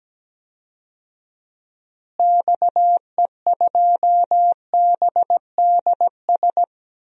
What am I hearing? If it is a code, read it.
Morse code, heard as a keyed tone, about 17 wpm, 700 Hz. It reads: XE2BDS
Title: DXCC-100 Call Signs - 6 Characters - Repeated 3x - 17wpm